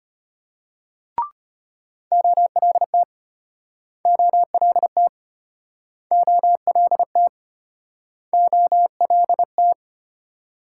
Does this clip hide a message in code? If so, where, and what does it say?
Yes: Morse code from 2.1 to 9.7 s, heard as a keyed tone, about 35 wpm, 700 Hz. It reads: OLT OLT OLT OLT